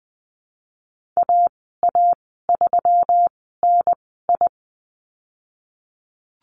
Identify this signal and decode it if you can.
Morse code, heard as a keyed tone, about 20 wpm, spaced as for 14 wpm, 700 Hz. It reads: AA3NI
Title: 179 Lesson 29g - Slash - Call Signs 20wpm